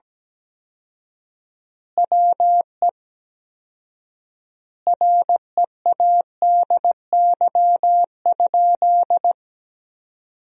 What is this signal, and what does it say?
Morse code, heard as a keyed tone, about 17 wpm, 700 Hz. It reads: WE READY?
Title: 4x Word-Spacing - Sentences from Top 400 Words - 17wpm